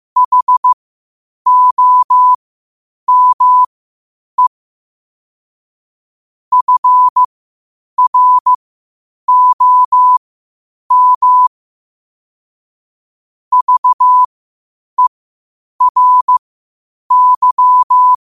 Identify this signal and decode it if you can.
Morse code, heard as a keyed tone, about 15 wpm, spaced as for 8 wpm, 1000 Hz. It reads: HOME FROM VERY
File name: qotdfiles / 8wpm